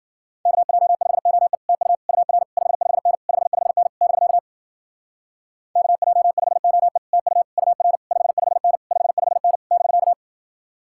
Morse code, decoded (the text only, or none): KJ5OE TU UR 55N 55N <BK> KJ5OE TU UR 55N 55N <BK>